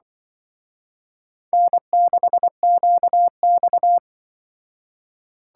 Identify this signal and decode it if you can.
Morse code, heard as a keyed tone, about 25 wpm, 700 Hz. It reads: N6QX